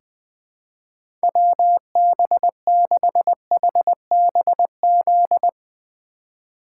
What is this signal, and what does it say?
Morse code, heard as a keyed tone, about 21 wpm, 700 Hz. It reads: WB6HBZ
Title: USA Call Signs - 6 Characters - 4x Character Spacing - 20wpm